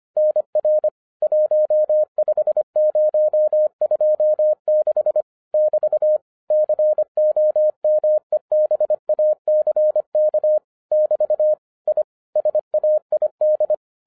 Morse code, heard as a keyed tone, about 25 wpm, 610 Hz. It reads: NR 15026 <BT> COMEBACK <BT> I SAID